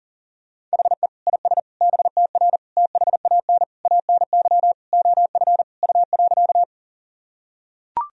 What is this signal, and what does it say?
Morse code, heard as a keyed tone, about 38 wpm, 700 Hz. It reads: HE IS BTR THAN ANY OF U.
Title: Sentences from Top 400 Words and Abbreviations 40wpm